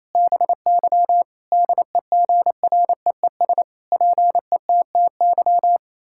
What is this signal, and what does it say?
Morse code, heard as a keyed tone, about 28 wpm, 700 Hz. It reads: BY DEGREES PETTY
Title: Pangram Challenge v2 - 28wpm